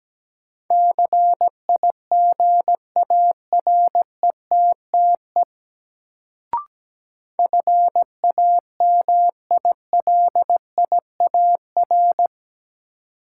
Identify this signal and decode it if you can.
Morse code, heard as a keyed tone, about 17 wpm, 700 Hz. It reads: CIGARETTE FAMILIAR
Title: Sentences from Top 1000 Words - New Word Drill - 17wpm